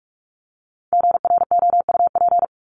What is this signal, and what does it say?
Morse code, heard as a keyed tone, about 44 wpm, 700 Hz. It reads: GROUP
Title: ES:S Suffix - Top 100 Words with Base Words 45wpm